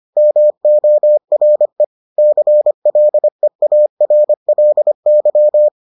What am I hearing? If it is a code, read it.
Morse code, heard as a keyed tone, about 25 wpm, 600 Hz. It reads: MORE CLEARLY